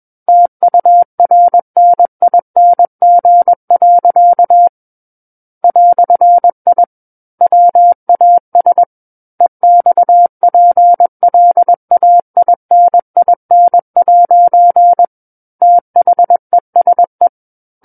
Morse code, heard as a keyed tone, about 21 wpm, 700 Hz. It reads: TURNING. "I WAS EXPLAININ' THESE
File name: Book One / 12-Chapter 12 - 21 WPM